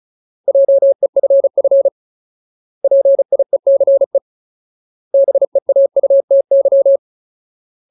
Morse code, heard as a keyed tone, about 34 wpm, 550 Hz. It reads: JEFF PIECE BEAUTY
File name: Ultimate Word List / 35WPM Ultimate